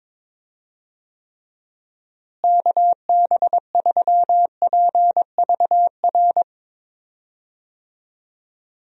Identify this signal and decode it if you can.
Morse code, heard as a keyed tone, about 22 wpm, 700 Hz. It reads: KB3PVR
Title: USA Call Signs - 6 Characters - Repeated 2x - 22wpm